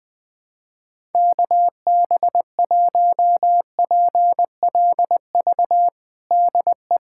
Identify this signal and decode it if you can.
Morse code, heard as a keyed tone, about 21 wpm, 700 Hz. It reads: KB1PLV DE
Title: USA Callsign DE Callsign - 6 Characters - 20wpm